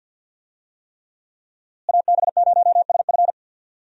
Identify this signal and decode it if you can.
Morse code, heard as a keyed tone, about 52 wpm, 700 Hz. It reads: AB0SF
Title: USA Call Signs - 5 Characters - Repeated 3x - 50wpm